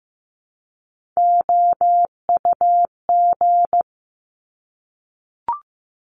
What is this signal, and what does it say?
Morse code, heard as a keyed tone, about 15 wpm, 700 Hz. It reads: OUG
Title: Top Three Letter Combinations in Words 15wpm